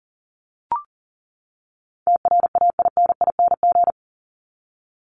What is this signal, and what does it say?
Morse code, heard as a keyed tone, about 40 wpm, 700 Hz. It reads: TRAINING
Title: Top 501-750 Words 40wpm